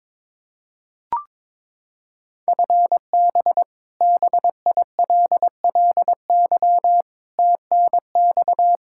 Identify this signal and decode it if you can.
Morse code, heard as a keyed tone, about 22 wpm, 700 Hz. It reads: FB BILLY TNX
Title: K1USN SST QSO Practice 22wpm